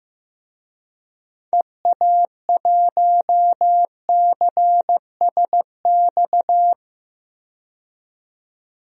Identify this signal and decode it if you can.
Morse code, heard as a keyed tone, about 16 wpm, 700 Hz. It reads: EA1CSX
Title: DXCC-100 Call Signs - 6 Characters - 15wpm